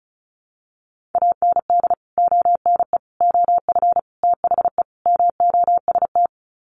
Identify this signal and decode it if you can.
Morse code, heard as a keyed tone, about 34 wpm, 700 Hz. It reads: AND ONE OF THE MOST
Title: Sets of 5 Words Encore 35wpm